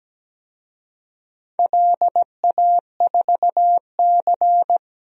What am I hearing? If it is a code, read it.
Morse code, heard as a keyed tone, about 17 wpm, 700 Hz. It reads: LA4C
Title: DXCC-100 Call Signs - 4 Characters - Repeated 2x - 17wpm